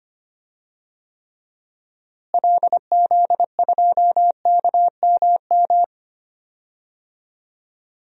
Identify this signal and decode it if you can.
Morse code, heard as a keyed tone, about 25 wpm, 700 Hz. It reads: LZ2KMM
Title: DXCC-100 Call Signs - 6 Characters - Repeated 2x - 25wpm